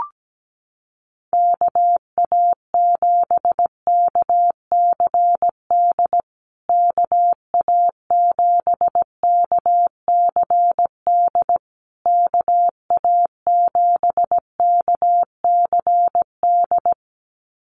Morse code, heard as a keyed tone, about 17 wpm, 700 Hz. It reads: KA7KCD KA7KCD KA7KCD
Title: USA Call Signs Repeated 3x 17wpm